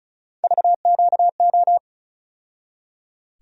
Morse code, heard as a keyed tone, about 36 wpm, 700 Hz. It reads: VQO
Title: USA Call Sign Postfixes - 3 Letters 35wpm